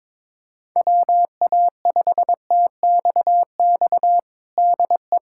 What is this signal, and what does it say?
Morse code, heard as a keyed tone, about 22 wpm, 700 Hz. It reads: WA5TXX DE
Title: USA Callsign DE Callsign - 6 Characters - 22wpm